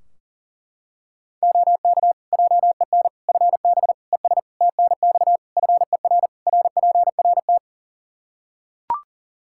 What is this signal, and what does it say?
Morse code, heard as a keyed tone, about 37 wpm, 700 Hz. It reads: OK JEN FB ES TNX FER RPRT
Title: Practice QSO - Exchange 2 - Info - 40wpm